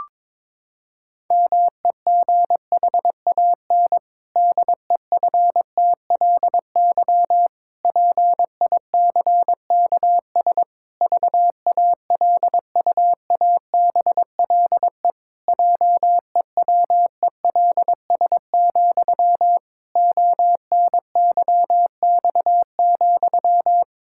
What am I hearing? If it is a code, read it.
Morse code, heard as a keyed tone, about 22 wpm, 700 Hz. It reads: MEGHAN DEFTLY PICKS VALUABLE JEWELS, ONYX,
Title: Pangram Challenge v2 - 22wpm